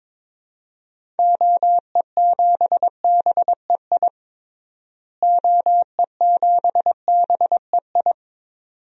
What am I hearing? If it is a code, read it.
Morse code, heard as a keyed tone, about 21 wpm, 700 Hz. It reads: OE7BEI OE7BEI
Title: DXCC-100 Call Signs - 6 Characters - Repeated 2x - 22wpm